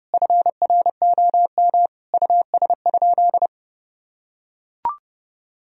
Morse code, heard as a keyed tone, about 30 wpm, 700 Hz. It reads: FROM US?
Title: Sentences from Top 1000 Words - Review Part 4 - 30wpm